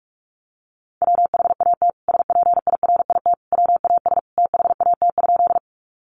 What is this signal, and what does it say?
Morse code, heard as a keyed tone, about 44 wpm, 700 Hz. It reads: WHAT SPIRIT WAS THAT?
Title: Sentences from Top 1400 Words 45wpm